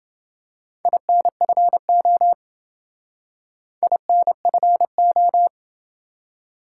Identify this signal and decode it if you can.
Morse code, heard as a keyed tone, about 27 wpm, 700 Hz. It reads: INFO INFO